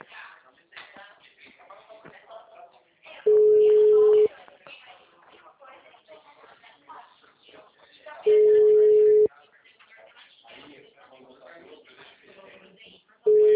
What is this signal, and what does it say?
Ringback tone (Europe, 425 Hz), with speech.
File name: out- -2006- - - .55668